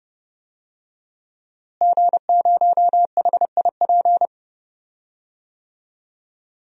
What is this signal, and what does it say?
Morse code, heard as a keyed tone, about 32 wpm, 700 Hz. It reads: G0HIP
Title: DXCC-100 Call Signs - 5 Characters - Repeated 3x - 30wpm